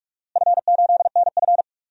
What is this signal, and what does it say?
Morse code, heard as a keyed tone, about 46 wpm, 700 Hz. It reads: F8NF
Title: DXCC-100 Call Signs - 4 Characters - Repeated 3x - 45wpm